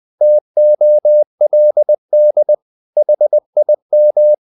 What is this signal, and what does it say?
Morse code, heard as a keyed tone, about 20 wpm, 600 Hz. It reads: TOLD HIM
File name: the old man and the sea / the old man and the sea part 1 20wpm